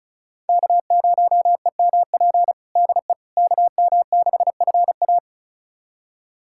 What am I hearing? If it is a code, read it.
Morse code, heard as a keyed tone, about 36 wpm, 700 Hz. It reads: K0EMP DE KM6FA
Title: USA Callsign DE Callsign - 5 Characters - 35wpm